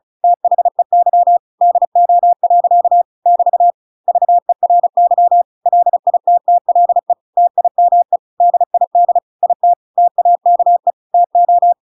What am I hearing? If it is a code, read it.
Morse code, heard as a keyed tone, about 34 wpm, 700 Hz. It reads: THEY DO. <BT> VERY LITTLE TIME DID IT TAKE TO